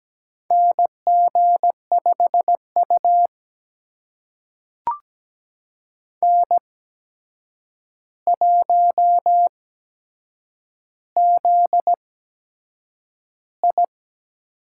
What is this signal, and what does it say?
Morse code, heard as a keyed tone, about 17 wpm, spaced as for 7 wpm, 700 Hz. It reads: NG5U N1ZI
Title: USA Call Signs - 4 Characters - 8x Character Spacing - 17wpm